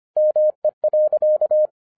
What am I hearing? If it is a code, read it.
Morse code, heard as a keyed tone, about 25 wpm, 610 Hz. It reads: ME.
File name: 2011-07-mp3 / n24 QRP OTAKU 25wpm